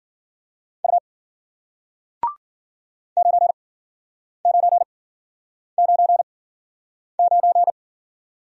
Morse code, heard as a keyed tone, about 48 wpm, spaced as for 9 wpm, 700 Hz. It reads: U 9999
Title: Speed-Racing Single Letter-Number 40wpm